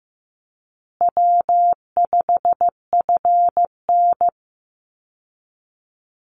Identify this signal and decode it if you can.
Morse code, heard as a keyed tone, about 15 wpm, 700 Hz. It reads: W5FN
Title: Call Signs Repeated 2x - 15wpm